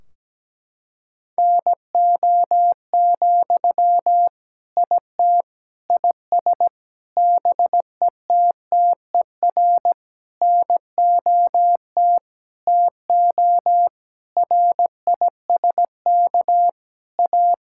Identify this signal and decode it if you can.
Morse code, heard as a keyed tone, about 17 wpm, 700 Hz. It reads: NO, IT IS BETTER NOT TO RISK A
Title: Sentences from Top 1600 Words 17wpm